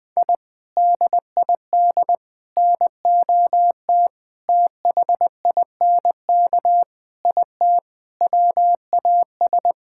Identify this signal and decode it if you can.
Morse code, heard as a keyed tone, about 21 wpm, 700 Hz. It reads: I DID NOT THINK IT WAS
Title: Sentences with 4-Word Phrases - 20wpm